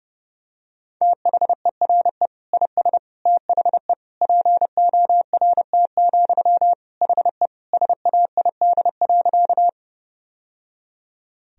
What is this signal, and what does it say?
Morse code, heard as a keyed tone, about 30 wpm, 700 Hz. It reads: THERE IS THE PORT, HE SAID.
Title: Sentences from Top 1000 Words - Review Part 1 - 30wpm